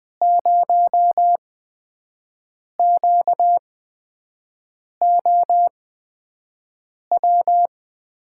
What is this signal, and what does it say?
Morse code, heard as a keyed tone, about 21 wpm, spaced as for 5 wpm, 700 Hz. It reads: 0QOW
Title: USA Call Signs - 5 Characters - 8x Character Spacing - 20wpm